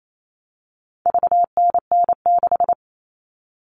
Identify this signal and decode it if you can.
Morse code, heard as a keyed tone, about 28 wpm, 700 Hz. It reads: VNN6
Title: All Cut Numbers - 4-Digit 28wpm